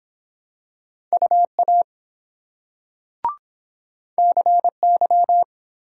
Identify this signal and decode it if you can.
Morse code, heard as a keyed tone, about 23 wpm, 700 Hz. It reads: UA CY